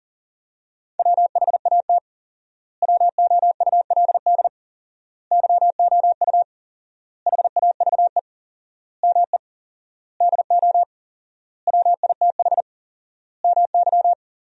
Morse code, heard as a keyed tone, about 37 wpm, 700 Hz. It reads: WHAT WOULD YOU HAVE ME DO WITH MY
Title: 4x Word-Spacing - Sentences from Top 300 Words - 40wpm